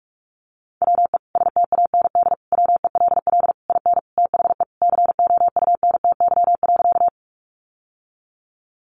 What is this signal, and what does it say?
Morse code, heard as a keyed tone, about 44 wpm, 700 Hz. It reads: WE STAND WELL IN THE COUNTY.
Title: Sentences from Top 1200 Words 45wpm